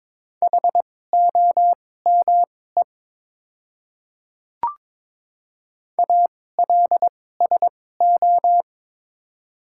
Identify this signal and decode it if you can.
Morse code, heard as a keyed tone, about 22 wpm, spaced as for 16 wpm, 700 Hz. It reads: HOME ALSO
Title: 2x Character Spacing - Top 200 Words - 22wpm